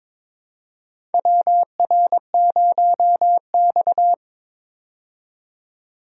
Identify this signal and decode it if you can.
Morse code, heard as a keyed tone, about 21 wpm, 700 Hz. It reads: WR0X
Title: USA Call Signs - 4 Characters - 22wpm